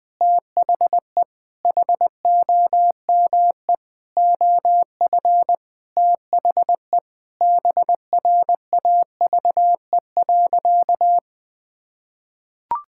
Morse code, heard as a keyed tone, about 20 wpm, 700 Hz. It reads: THE HOME OF THE BRAVE.
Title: Sentences with 9-Word Phrases - 20wpm